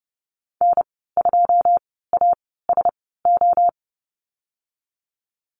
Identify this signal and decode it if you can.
Morse code, heard as a keyed tone, about 30 wpm, spaced as for 17 wpm, 700 Hz. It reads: N2ASO
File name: Lesson 19g - Letter w - Call Signs 30wpm